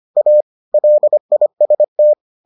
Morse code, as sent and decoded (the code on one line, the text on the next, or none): .- / .-.. .. ... -
A LIST